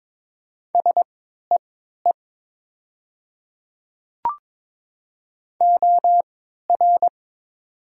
Morse code, heard as a keyed tone, about 22 wpm, spaced as for 13 wpm, 700 Hz. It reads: SEE OR
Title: 3x Character Spacing - Top 100 Words - 22wpm